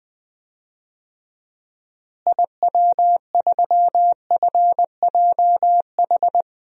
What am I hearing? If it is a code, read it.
Morse code, heard as a keyed tone, about 20 wpm, 700 Hz. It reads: IW3FJH
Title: DXCC-100 Call Signs - 6 Characters - Repeated 3x - 20wpm